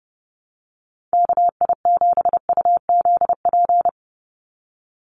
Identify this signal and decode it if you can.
Morse code, heard as a keyed tone, about 30 wpm, 700 Hz. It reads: KI7UZP
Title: USA Call Signs 30wpm